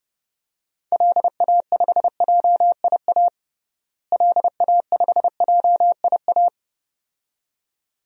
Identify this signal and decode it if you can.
Morse code, heard as a keyed tone, about 30 wpm, 700 Hz. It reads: LA5JIA LA5JIA
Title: DXCC-100 Call Signs - 6 Characters - Repeated 2x - 30wpm